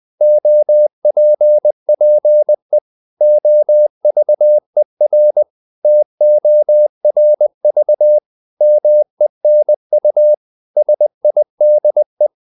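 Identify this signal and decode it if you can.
Morse code, heard as a keyed tone, about 20 wpm, 600 Hz. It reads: OPPE OVER TORV MENU SIDE
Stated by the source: Morse code practice of short Danish words at 20 wpm, from CW med Gnister.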